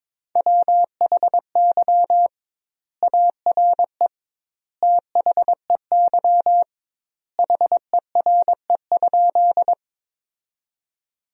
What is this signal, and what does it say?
Morse code, heard as a keyed tone, about 21 wpm, 700 Hz. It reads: WHY ARE THEY HERE?
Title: 2x Word-Spacing - Sentences from Top 200 Words - 22wpm